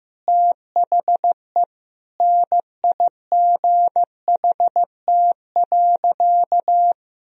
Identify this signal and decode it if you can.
Morse code, heard as a keyed tone, about 16 wpm, 700 Hz. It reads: THE NIGHT.
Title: Sentences from Top 1000 Words - Review Part 2 - 15wpm